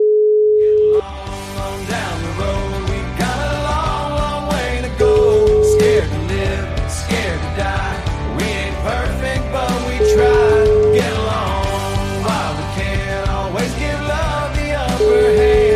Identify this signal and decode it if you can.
Ringback tone (Europe, 425 Hz), with noise.